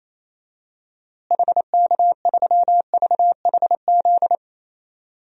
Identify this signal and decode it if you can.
Morse code, heard as a keyed tone, about 28 wpm, 700 Hz. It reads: HK3VHZ
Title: DXCC-100 Call Signs - 6 Characters - Repeated 2x - 28wpm